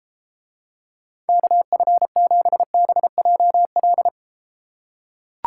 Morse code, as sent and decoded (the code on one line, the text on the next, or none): -.- ..-. --... -... .--- .-..
KF7BJL